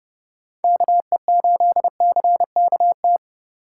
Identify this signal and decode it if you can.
Morse code, heard as a keyed tone, about 30 wpm, 700 Hz. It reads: KE8CKT